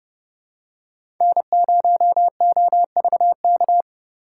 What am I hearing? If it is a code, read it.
Morse code, heard as a keyed tone, about 32 wpm, 700 Hz. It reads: N0OVK
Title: USA Call Signs - 4x Character Spacing - 30wpm